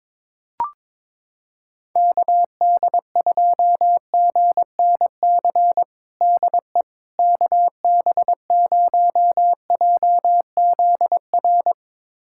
Morse code, heard as a keyed tone, about 22 wpm, 700 Hz. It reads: KD2GNC DE KB0JZR
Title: USA Callsign DE Callsign - 6 Characters - 22wpm